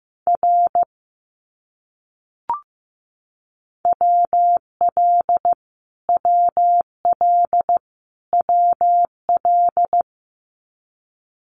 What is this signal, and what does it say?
Morse code, heard as a keyed tone, about 15 wpm, 700 Hz. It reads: R WL WL WL
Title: Abbreviations Repeated 3x - 15wpm